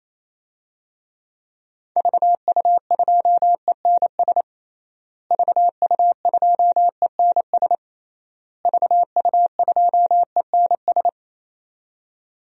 Morse code, heard as a keyed tone, about 28 wpm, 700 Hz. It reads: VU2ENS VU2ENS VU2ENS
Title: DXCC-100 Call Signs - 6 Characters - Repeated 3x - 28wpm